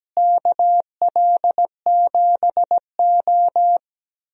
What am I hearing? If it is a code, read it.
Morse code, heard as a keyed tone, about 17 wpm, 700 Hz. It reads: KL7O
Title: DXCC-100 Call Signs - 4 Characters - Repeated 3x - 17wpm